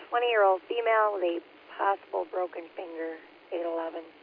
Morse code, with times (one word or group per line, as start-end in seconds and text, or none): none